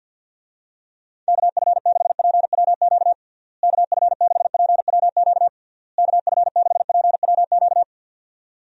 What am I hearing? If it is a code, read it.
Morse code, heard as a keyed tone, about 52 wpm, 700 Hz. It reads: KF6PWQ KF6PWQ KF6PWQ
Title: USA Call Signs - 6 Characters - Repeated 3x - 50wpm